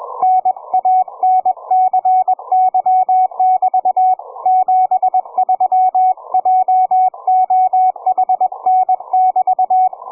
Morse code, heard as a keyed tone, about 21 wpm, 740 Hz, with radio noise. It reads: NANCY<BT> 73 JOHN<BT>
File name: cw forum / ve1zj 1830 160m 04h00